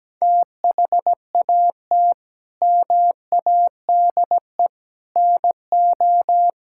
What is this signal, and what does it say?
Morse code, heard as a keyed tone, about 17 wpm, 700 Hz. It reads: THAT MADE NO
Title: Sentences from Top 1000 Words - Review Part 5 - 17wpm